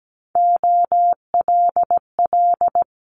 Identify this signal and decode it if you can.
Morse code, heard as a keyed tone, about 17 wpm, 700 Hz. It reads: OLL
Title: Multi-letter Phonemes Quick Fire 17wpm